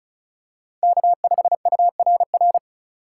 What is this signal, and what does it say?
Morse code, heard as a keyed tone, about 36 wpm, 700 Hz. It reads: K5URR
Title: USA Call Signs - 5 Characters - 6x Character Spacing - 35wpm